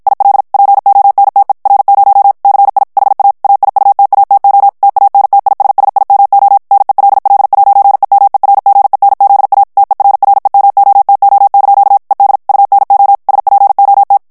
Morse code, heard as a keyed tone, about 53 wpm, 800 Hz. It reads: IL PONTE R0 CI HA AIUTATO TANTISSIMO NELL'EMERGENZA TERREMOTO) ED UNO SPOT
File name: ITALIEN texte clair 1 vitesse 50 mots minute